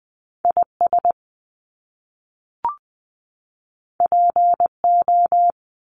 Morse code, heard as a keyed tone, about 20 wpm, 700 Hz. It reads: I S P O
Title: Top Two Letter Combinations in Words 20wpm